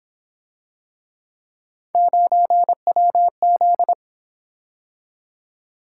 Morse code, heard as a keyed tone, about 26 wpm, 700 Hz. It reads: 9WZ
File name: Speed-Racing - Three Letter-Number Combinations - 22wpm